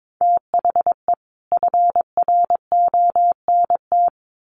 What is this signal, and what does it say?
Morse code, heard as a keyed tone, about 22 wpm, 700 Hz. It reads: THE FRONT
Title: Sets of 4 Words 22wpm